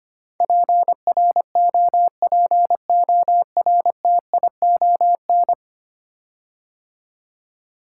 Morse code, heard as a keyed tone, about 25 wpm, 700 Hz. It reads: PROPORTION
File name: 10 Letter Words - First 3 Letters Given - 25wpm